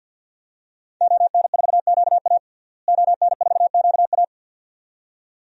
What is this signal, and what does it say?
Morse code, heard as a keyed tone, about 50 wpm, 700 Hz. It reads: ON4QA ON4QA
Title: DXCC-100 Call Signs - 5 Characters - Repeated 2x - 50wpm